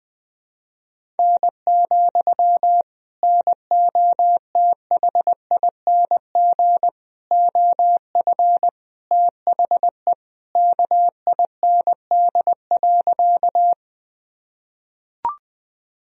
Morse code, heard as a keyed tone, about 21 wpm, 700 Hz. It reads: N, NOTHING OF THE KIND.
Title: Sentences from Top 500 Words and Abbreviations 20wpm